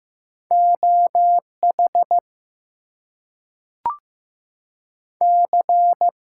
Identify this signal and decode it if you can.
Morse code, heard as a keyed tone, about 15 wpm, 700 Hz. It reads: OH C
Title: US States and Canadian Provinces 15wpm